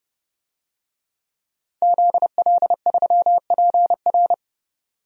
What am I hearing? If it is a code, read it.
Morse code, heard as a keyed tone, about 30 wpm, 700 Hz. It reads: ZL3PR
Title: DXCC-100 Call Signs - 5 Characters - 30wpm